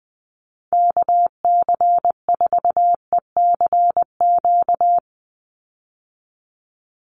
Morse code, heard as a keyed tone, about 20 wpm, 700 Hz. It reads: KC4ECQ
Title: USA Call Signs 20wpm